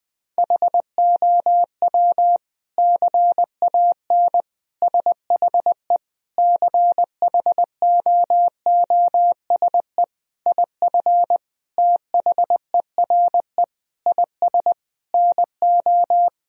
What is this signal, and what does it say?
Morse code, heard as a keyed tone, about 20 wpm, 700 Hz. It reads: HOW CAN SHE CHOOSE IF THERE IS NO
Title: Sentences from Top 1000 Words - Review Part 4 - 20wpm